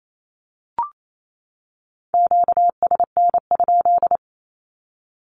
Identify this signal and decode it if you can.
Morse code, heard as a keyed tone, about 28 wpm, 700 Hz. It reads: QSN?
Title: Q-Codes 28wpm